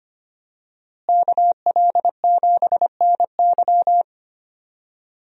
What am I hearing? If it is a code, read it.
Morse code, heard as a keyed tone, about 25 wpm, 700 Hz. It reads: KL7NY
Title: USA Call Signs - 5 Characters - Repeated 3x - 25wpm